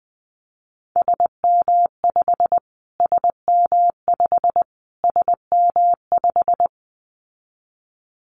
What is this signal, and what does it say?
Morse code, heard as a keyed tone, about 20 wpm, 700 Hz. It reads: SM5 SM5 SM5
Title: Callsign Prefixes for Top 25 DXCC - Repeated 3x - 20wpm